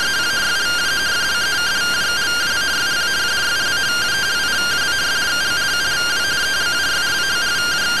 Two-tone teleprinter signal, about 170 Hz shift.